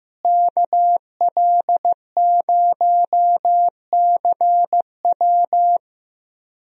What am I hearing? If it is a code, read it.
Morse code, heard as a keyed tone, about 15 wpm, 700 Hz. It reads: KL0CW
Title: USA Call Signs - 5 Characters - Repeated 3x - 15wpm